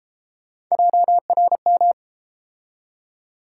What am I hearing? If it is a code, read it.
Morse code, heard as a keyed tone, about 30 wpm, 700 Hz. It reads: JRM